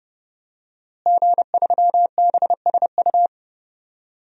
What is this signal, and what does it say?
Morse code, heard as a keyed tone, about 30 wpm, 700 Hz. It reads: G3BSU
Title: DXCC-100 Call Signs - 5 Characters - Repeated 3x - 30wpm